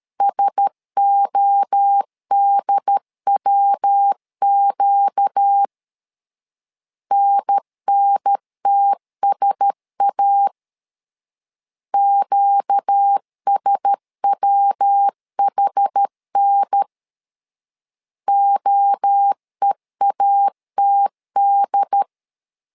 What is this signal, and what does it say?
Morse code, heard as a keyed tone, about 13 wpm, 800 Hz. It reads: SODWQ NNTSA QSWHN OEATD
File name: LEZIONI / 4 Q1S 12wpm 1802